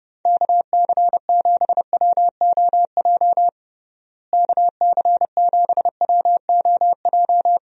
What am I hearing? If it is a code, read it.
Morse code, heard as a keyed tone, about 30 wpm, 700 Hz. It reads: KC7WOJ KC7WOJ
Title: USA Call Signs - 6 Characters - Repeated 3x - 30wpm